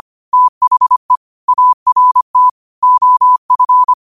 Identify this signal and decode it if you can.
Morse code, heard as a keyed tone, about 25 wpm, 1000 Hz. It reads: THE ART OF